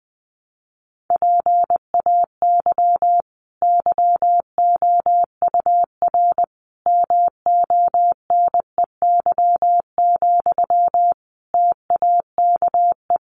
Morse code, heard as a keyed tone, about 20 wpm, 700 Hz. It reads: PAY YOUR MONEY, TAKE
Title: Sentences from Top 900 Words 20wpm